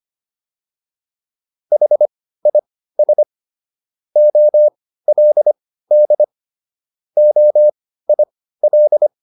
Morse code, heard as a keyed tone, about 25 wpm, spaced as for 15 wpm, 600 Hz. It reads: HIS OLD OIL